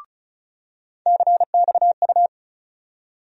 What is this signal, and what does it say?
Morse code, heard as a keyed tone, about 34 wpm, 700 Hz. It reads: CXU
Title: USA Call Sign Postfixes - 3 Letters 35wpm